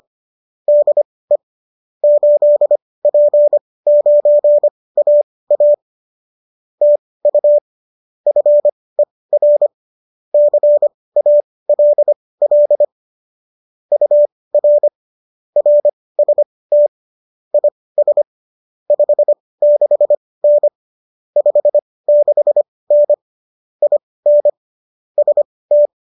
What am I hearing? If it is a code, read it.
Morse code, heard as a keyed tone, about 25 wpm, spaced as for 18 wpm, 590 Hz. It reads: DE 8P9AA TU FER CALL UR RST IS 56N 56N IN ST